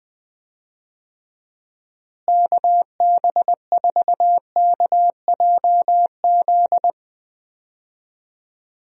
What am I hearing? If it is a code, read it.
Morse code, heard as a keyed tone, about 20 wpm, 700 Hz. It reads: KB4KJZ
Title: USA Call Signs - 6 Characters - Repeated 3x - 20wpm